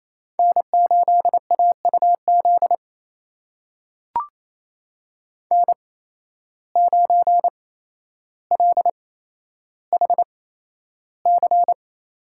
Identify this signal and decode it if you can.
Morse code, heard as a keyed tone, about 28 wpm, spaced as for 12 wpm, 700 Hz. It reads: N8AUZ N9LHC
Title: USA Call Signs - 5 Characters - 8x Character Spacing - 28wpm